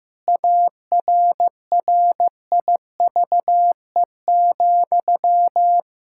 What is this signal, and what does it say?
Morse code, heard as a keyed tone, about 15 wpm, 700 Hz. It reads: ARRIVE,
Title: Pangram Challenge v2 - 15wpm